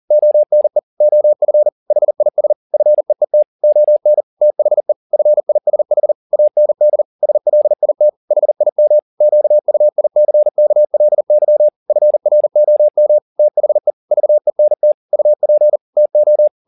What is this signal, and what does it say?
Morse code, heard as a keyed tone, about 40 wpm, 600 Hz. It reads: ONE OF HIS FEET ON THE FISH AND SLIT HIM QUICKLY FROM THE VENT UP TO
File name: the old man and the sea 2 40wpm